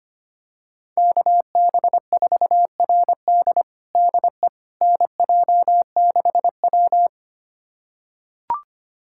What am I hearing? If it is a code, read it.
Morse code, heard as a keyed tone, about 25 wpm, 700 Hz. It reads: KB4RD DE NJ6W
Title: USA Callsign DE Callsign 25wpm